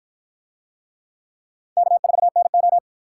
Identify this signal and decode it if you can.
Morse code, heard as a keyed tone, about 53 wpm, 700 Hz. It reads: K4NO